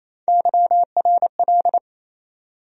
Morse code, heard as a keyed tone, about 28 wpm, 700 Hz. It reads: YRL
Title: USA Call Sign Postfixes - 3 Letters 28wpm